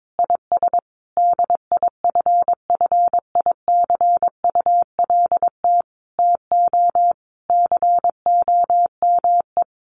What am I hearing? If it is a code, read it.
Morse code, heard as a keyed tone, about 22 wpm, 700 Hz. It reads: IS DIFFICULT TO COME
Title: Sentences from Top 1500 Words 22wpm